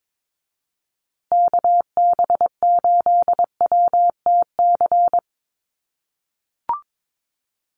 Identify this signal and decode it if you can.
Morse code, heard as a keyed tone, about 22 wpm, 700 Hz. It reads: KB8WTC
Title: USA Call Signs Repeated 2x 22wpm